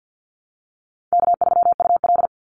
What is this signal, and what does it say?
Morse code, heard as a keyed tone, about 50 wpm, 700 Hz. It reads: K3UL
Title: Easy Call Signs - 50wpm